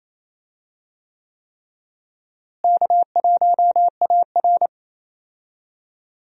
Morse code, heard as a keyed tone, about 27 wpm, 700 Hz. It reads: K1AR
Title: USA Call Signs - 4 Characters - Repeated 3x - 28wpm